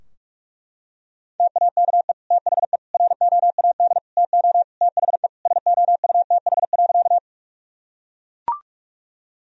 Morse code, heard as a keyed tone, about 41 wpm, 700 Hz. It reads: TAKE THE ROAD TO THE SOUTH.
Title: Sentences from Top 1000 Words - Review Part 2 - 45wpm